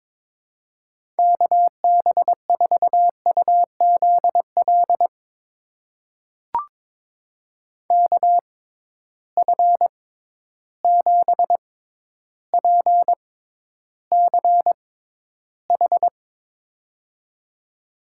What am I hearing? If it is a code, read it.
Morse code, heard as a keyed tone, about 22 wpm, spaced as for 11 wpm, 700 Hz. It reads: KB4UZL KF7PCH
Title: USA Call Signs - 6 Characters - 6x Character Spacing - 22wpm